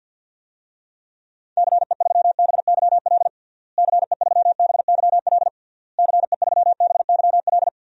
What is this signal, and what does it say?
Morse code, heard as a keyed tone, about 52 wpm, 700 Hz. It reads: CE3BYL CE3BYL CE3BYL
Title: DXCC-100 Call Signs - 6 Characters - Repeated 3x - 50wpm